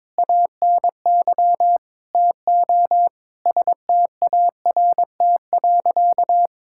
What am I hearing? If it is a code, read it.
Morse code, heard as a keyed tone, about 22 wpm, 700 Hz. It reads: ANY TO START.
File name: practice / 22wpm / Sentences with 8-Word Phrases - 22wpm